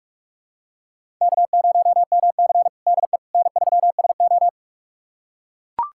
Morse code, heard as a keyed tone, about 44 wpm, 690 Hz. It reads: K0MC DE N3SO
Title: USA Callsign DE Callsign - 4 Characters - 45wpm